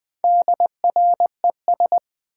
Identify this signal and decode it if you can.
Morse code, heard as a keyed tone, about 20 wpm, 700 Hz. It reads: DRES
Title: Build A Word - 1 Letter At A Time - 5-Letter Words - 20wpm